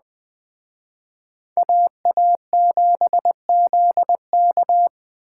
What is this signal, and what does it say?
Morse code, heard as a keyed tone, about 20 wpm, 700 Hz. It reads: AA7ZK
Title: Speed Racing - USA Call Signs 17wpm